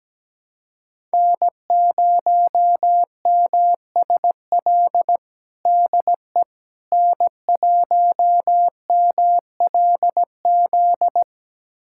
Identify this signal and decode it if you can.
Morse code, heard as a keyed tone, about 17 wpm, 700 Hz. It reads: N0MSL DE N1MLZ
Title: USA Callsign DE Callsign - 5 Characters - 17wpm